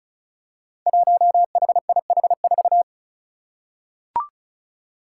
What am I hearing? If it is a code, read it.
Morse code, heard as a keyed tone, about 34 wpm, 700 Hz. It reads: 1HIH4
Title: Five Challenging Letter-Number Combinations 35wpm